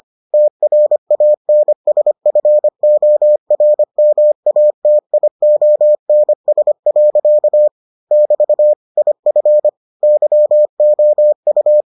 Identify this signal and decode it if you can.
Morse code, heard as a keyed tone, about 25 wpm, 600 Hz. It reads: TRANSFORMATIONS. <BT> IF YOU